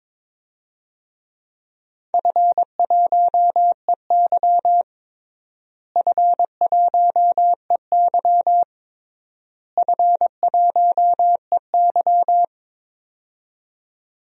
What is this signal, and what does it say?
Morse code, heard as a keyed tone, about 22 wpm, 700 Hz. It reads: F1EY F1EY F1EY
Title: DXCC-100 Call Signs - 4 Characters - Repeated 3x - 22wpm